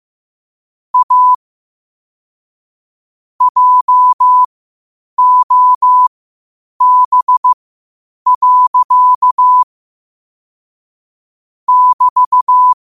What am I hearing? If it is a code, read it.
Morse code, heard as a keyed tone, about 15 wpm, spaced as for 8 wpm, 1000 Hz. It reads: A JOB. <BT>
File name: qotdfiles / 8wpm